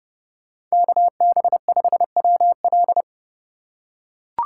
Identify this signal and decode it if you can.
Morse code, heard as a keyed tone, about 30 wpm, 700 Hz. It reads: KB5WL